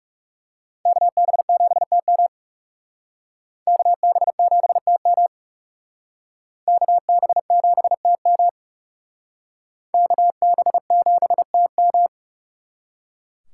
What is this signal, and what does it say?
Morse code, heard as a keyed tone, about 36 wpm, 700 Hz. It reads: KB7TM KB7TM KB7TM KB7TM
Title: Speed Racing - USA Call Signs 30wpm